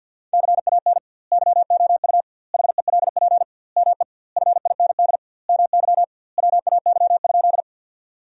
Morse code, heard as a keyed tone, about 50 wpm, 700 Hz. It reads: CAN YOU HELP ME FIND MY WAY?
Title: 2x Word-Spacing - Sentences from Top 200 Words - 50wpm